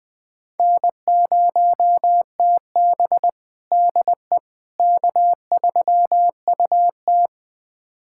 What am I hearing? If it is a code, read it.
Morse code, heard as a keyed tone, about 20 wpm, 700 Hz. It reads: N0TB DE K3UT
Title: USA Callsign DE Callsign - 4 Characters - 20wpm